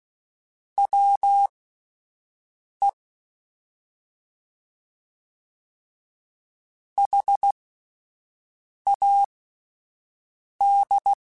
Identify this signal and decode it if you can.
Morse code, heard as a keyed tone, about 16 wpm, spaced as for 5 wpm, 800 Hz. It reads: WE HAD
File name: Spaced Characters / Spaced Characters-mp3s / Weather4 24 sp @10